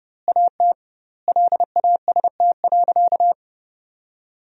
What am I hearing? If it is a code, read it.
Morse code, heard as a keyed tone, about 31 wpm, 700 Hz. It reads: AT LAST.
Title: 2x Word-Spacing - Sentences from Top 300 Words - 30wpm